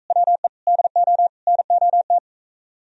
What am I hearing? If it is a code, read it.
Morse code, heard as a keyed tone, about 45 wpm, 700 Hz. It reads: WE DO NOT